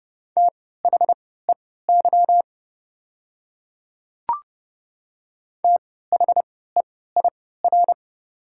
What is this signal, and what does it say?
Morse code, heard as a keyed tone, about 30 wpm, spaced as for 17 wpm, 690 Hz. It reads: THEY THEIR
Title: 3x Character Spacing - Top 100 Words - 30wpm